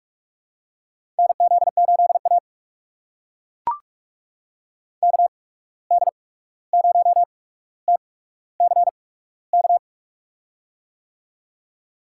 Morse code, heard as a keyed tone, about 42 wpm, spaced as for 17 wpm, 700 Hz. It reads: NZ8A KD0TCK
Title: USA Call Signs - 8x Character Spacing - 45wpm